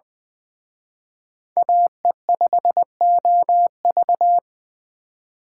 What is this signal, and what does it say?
Morse code, heard as a keyed tone, about 24 wpm, 700 Hz. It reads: AE5OV
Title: Speed Racing - USA 2x2 Call Signs 20wpm